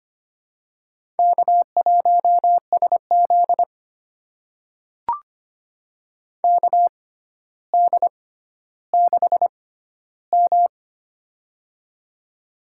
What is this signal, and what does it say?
Morse code, heard as a keyed tone, about 25 wpm, spaced as for 13 wpm, 700 Hz. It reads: K1SZ KD6M
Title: USA Call Signs - 4 Characters - 6x Character Spacing - 25wpm